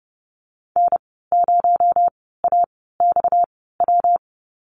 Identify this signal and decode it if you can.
Morse code, heard as a keyed tone, about 30 wpm, spaced as for 17 wpm, 700 Hz. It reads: N0AXW
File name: Lesson 37g - Letter x - Call Signs 30wpm